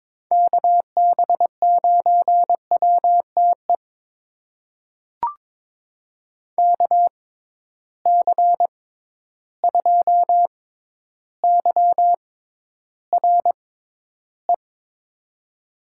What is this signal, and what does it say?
Morse code, heard as a keyed tone, about 22 wpm, spaced as for 11 wpm, 700 Hz. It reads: KB9WTE KC2YRE
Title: USA Call Signs - 6 Characters - 6x Character Spacing - 22wpm